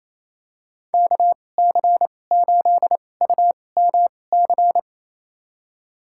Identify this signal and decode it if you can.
Morse code, heard as a keyed tone, about 28 wpm, spaced as for 20 wpm, 700 Hz. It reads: KC8UMC
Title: USA Call Signs - 6 Characters - 2x Character Spacing - 28wpm